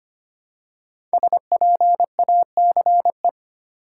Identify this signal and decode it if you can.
Morse code, heard as a keyed tone, about 25 wpm, 700 Hz. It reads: SPACE
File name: 5 Letter Words - First 3 Letters Given - 25wpm